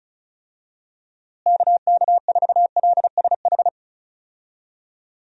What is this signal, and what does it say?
Morse code, heard as a keyed tone, about 36 wpm, 700 Hz. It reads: KK4LSH
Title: USA Call Signs - 6 Characters - Repeated 3x - 35wpm